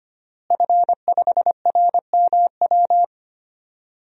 Morse code, heard as a keyed tone, about 25 wpm, 700 Hz. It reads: F5RMW
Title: DXCC-100 Call Signs - 5 Characters - Repeated 3x - 25wpm